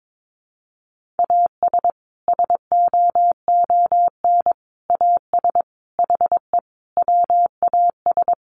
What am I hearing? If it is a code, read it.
Morse code, heard as a keyed tone, about 22 wpm, 700 Hz. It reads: AS SOON AS HE WAS
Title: Sentences from Top 1500 Words 22wpm